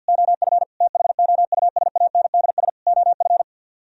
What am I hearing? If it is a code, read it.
Morse code, heard as a keyed tone, about 62 wpm, 700 Hz. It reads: OF THOUSANDS OF
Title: Speed-Racing - Sets of 4 Words 50wpm